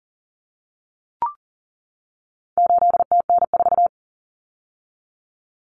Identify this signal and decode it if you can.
Morse code, heard as a keyed tone, about 40 wpm, 700 Hz. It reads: 8TN4
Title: Common Cut Numbers - 4-Digit 40wpm